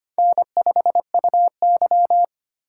Morse code, heard as a keyed tone, about 25 wpm, 690 Hz. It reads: N5UY
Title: USA Call Signs - 4 Characters - 4x Character Spacing - 25wpm